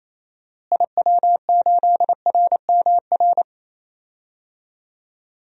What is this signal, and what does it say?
Morse code, heard as a keyed tone, about 28 wpm, 700 Hz. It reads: IW8RMR
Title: DXCC-100 Call Signs - 6 Characters - Repeated 3x - 28wpm